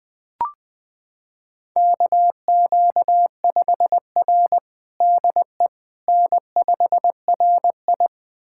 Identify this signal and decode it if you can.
Morse code, heard as a keyed tone, about 20 wpm, 700 Hz. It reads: KQ5R DE N5RI
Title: USA Callsign DE Callsign - 4 Characters - 20wpm